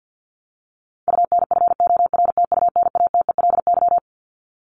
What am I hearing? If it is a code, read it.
Morse code, heard as a keyed tone, about 50 wpm, 700 Hz. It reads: UNFORTUNATELY
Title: LY Suffix - Top 100 Words with Base Words 50wpm